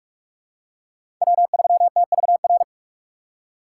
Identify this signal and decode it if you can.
Morse code, heard as a keyed tone, about 46 wpm, 700 Hz. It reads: W3TVR
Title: USA Call Signs - 5 Characters - 2x Character Spacing - 45wpm